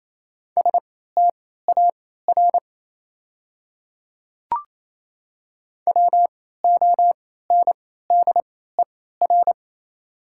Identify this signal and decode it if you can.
Morse code, heard as a keyed tone, about 28 wpm, spaced as for 16 wpm, 690 Hz. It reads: STAR WONDER
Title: 3x Character Spacing - Top 500 Words - 28wpm